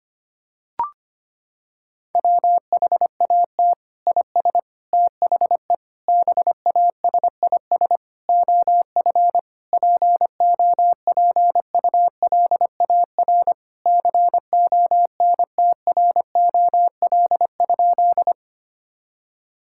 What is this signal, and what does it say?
Morse code, heard as a keyed tone, about 25 wpm, 700 Hz. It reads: WHAT IS THE BASIS OF POPULAR CONTROL?
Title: Sentences from Top 1000 Words - Review Part 5 - 25wpm